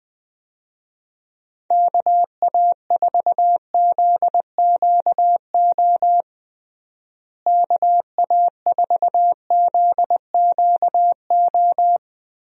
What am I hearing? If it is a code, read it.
Morse code, heard as a keyed tone, about 21 wpm, 700 Hz. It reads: KA4ZQO KA4ZQO
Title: USA Call Signs - 6 Characters - Repeated 3x - 20wpm